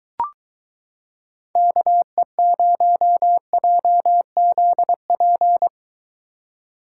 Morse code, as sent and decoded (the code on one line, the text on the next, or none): -.- . ----- .--- --.. .--.
KE0JZP